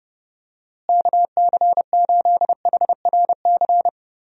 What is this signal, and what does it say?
Morse code, heard as a keyed tone, about 32 wpm, 700 Hz. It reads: KC8HRC
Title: USA Call Signs - 6 Characters - Repeated 3x - 30wpm